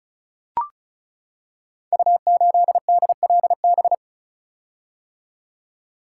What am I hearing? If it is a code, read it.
Morse code, heard as a keyed tone, about 36 wpm, 700 Hz. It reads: U8DLB
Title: Code Groups - 5 Letter:Numbers - 35wpm